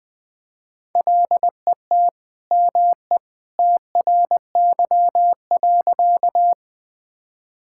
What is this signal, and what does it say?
Morse code, heard as a keyed tone, about 21 wpm, 700 Hz. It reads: LET ME TRY.
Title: Sentences with 2-Word Phrases - Spoken Beginning - 20wpm